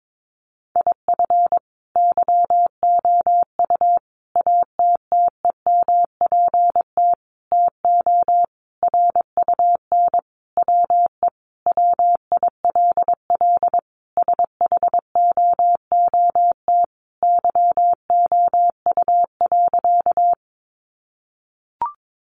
Morse code, heard as a keyed tone, about 22 wpm, 700 Hz. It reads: IF YOU ATTEMPT TO RUN WE WILL SHOOT YOU.
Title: Sentences from Top 1200 Words 22wpm